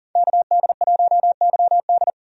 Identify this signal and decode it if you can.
Morse code, heard as a keyed tone, about 40 wpm, 700 Hz. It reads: KD1YD
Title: USA Call Signs - 5 Characters - 40wpm